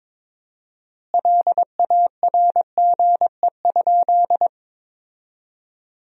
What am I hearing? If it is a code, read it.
Morse code, heard as a keyed tone, about 22 wpm, 700 Hz. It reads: LARGE?
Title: 4x Word-Spacing - Sentences from Top 200 Words - 22wpm